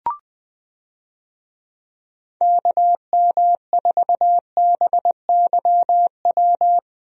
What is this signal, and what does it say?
Morse code, heard as a keyed tone, about 20 wpm, 700 Hz. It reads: KM4BYW
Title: USA Call Signs - 6 Characters - Repeated 3x - 20wpm